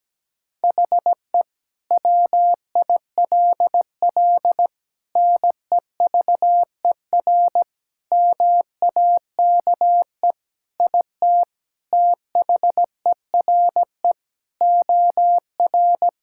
Morse code, heard as a keyed tone, about 17 wpm, 700 Hz. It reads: HE WILL NEVER MAKE IT THERE OR
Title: Sentences from Top 1600 Words 17wpm